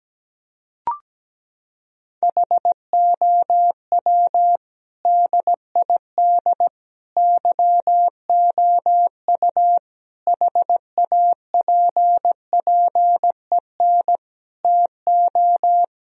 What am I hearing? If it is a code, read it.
Morse code, heard as a keyed tone, about 17 wpm, 700 Hz. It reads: HOW DID YOU HAPPEN TO
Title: Sentences from Top 1000 Words - Review Part 4 - 17wpm